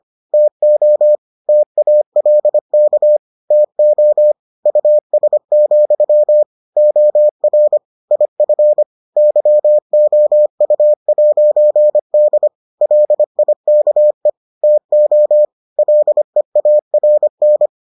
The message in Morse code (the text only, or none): TO TALK TO US, OR IF YOU'D LIKE TO LEARN